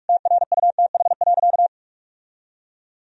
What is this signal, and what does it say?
Morse code, heard as a keyed tone, about 47 wpm, 700 Hz. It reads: TRUTH.
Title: Sentences from Top 1000 Words - Review Part 1 - 45wpm